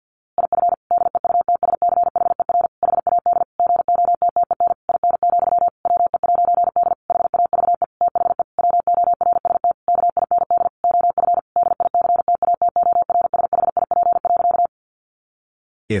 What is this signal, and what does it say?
Morse code, heard as a keyed tone, about 50 wpm, 700 Hz. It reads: IF DEUTSCHER HAD GOTTEN IN, WE'D HAVE THE WORST KIND OF DICTATORSHIP.
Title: A Sound Of Thunder 50wpm